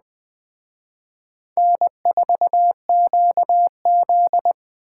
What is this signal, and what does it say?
Morse code, heard as a keyed tone, about 23 wpm, 700 Hz. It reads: N4QZ
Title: Speed Racing - USA 1x2 Call Signs 20wpm